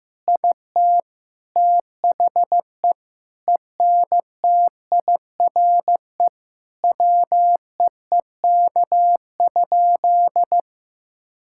Morse code, heard as a keyed tone, about 15 wpm, 700 Hz. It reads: IT THE ENTIRE WEEK?
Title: Sentences from Top 1000 Words - Review Part 5 - 15wpm